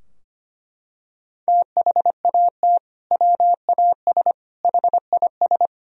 Morse code, heard as a keyed tone, about 25 wpm, 700 Hz. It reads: THAT WAS HIS
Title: Sentences from Top 1000 Words - Review Part 2 - 25wpm